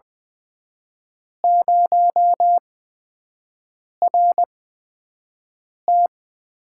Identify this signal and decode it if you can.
Morse code, heard as a keyed tone, about 20 wpm, spaced as for 5 wpm, 700 Hz. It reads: 0RT
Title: USA Call Signs - 4 Characters - 8x Character Spacing - 20wpm